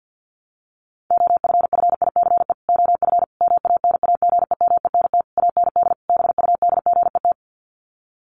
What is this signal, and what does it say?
Morse code, heard as a keyed tone, about 50 wpm, 700 Hz. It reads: OFFICE OF MANAGEMENT AND BUDGET
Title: Sets of 5 Words Encore2 50wpm